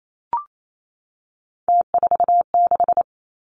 Morse code, heard as a keyed tone, about 28 wpm, 700 Hz. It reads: T46
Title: Three Challenging Letter-Number Combinations 28wpm